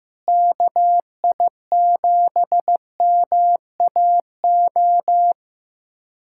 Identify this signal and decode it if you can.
Morse code, heard as a keyed tone, about 15 wpm, 700 Hz. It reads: KI7MAO
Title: USA Callsign DE Callsign 15wpm